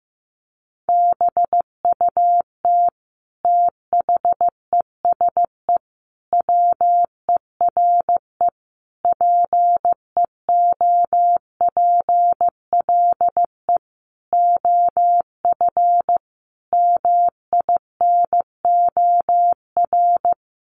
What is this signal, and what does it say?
Morse code, heard as a keyed tone, about 15 wpm, 700 Hz. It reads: BUT THESE WERE PEOPLE OF MINOR
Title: Sentences from Top 1500 Words 15wpm